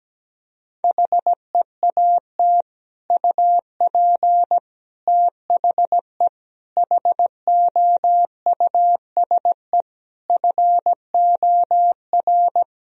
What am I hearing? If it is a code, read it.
Morse code, heard as a keyed tone, about 17 wpm, 700 Hz. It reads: HEAT UP THE HOUSE FOR
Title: Sentences from Top 1000 Words - Review Part 3 - 17wpm